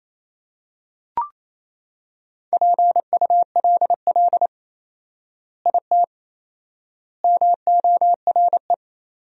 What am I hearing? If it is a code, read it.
Morse code, heard as a keyed tone, about 28 wpm, 700 Hz. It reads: PULL IT MORE
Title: 4x Word-Spacing - Sentences from Top 400 Words - 28wpm